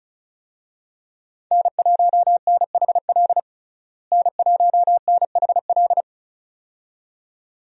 Morse code, heard as a keyed tone, about 34 wpm, 700 Hz. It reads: N1NHL N1NHL
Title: USA Call Signs - 5 Characters - Repeated 2x - 35wpm